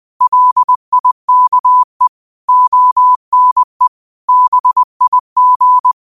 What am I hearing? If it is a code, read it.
Morse code, heard as a keyed tone, about 20 wpm, 1000 Hz. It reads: LIKE ONE BIG